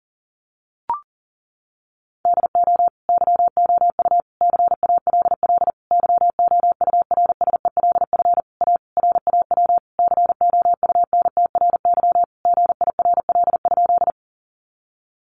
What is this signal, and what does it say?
Morse code, heard as a keyed tone, about 40 wpm, 700 Hz. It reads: DO YOU CALL YOURSELF A RAW COUNTRY GIRL?
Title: Sentences from Top 1500 Words 40wpm